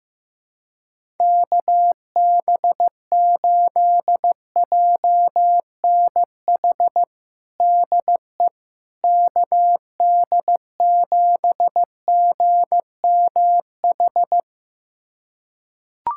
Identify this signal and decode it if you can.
Morse code, heard as a keyed tone, about 15 wpm, 700 Hz. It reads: KB8JNH DE KD7GMH
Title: USA Callsign DE Callsign - 6 Characters - 15wpm